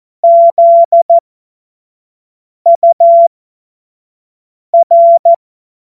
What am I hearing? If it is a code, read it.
Morse code, heard as a keyed tone, about 14 wpm, spaced as for 5 wpm, 680 Hz. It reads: ZUR